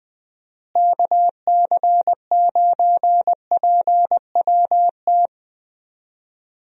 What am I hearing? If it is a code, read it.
Morse code, heard as a keyed tone, about 20 wpm, 700 Hz. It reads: KC9PWT
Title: USA Call Signs - 6 Characters - Repeated 2x - 20wpm